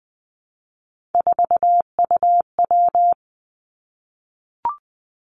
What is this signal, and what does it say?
Morse code, heard as a keyed tone, about 20 wpm, 700 Hz. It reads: 4UW
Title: Three Challenging Letter-Number Combinations 20wpm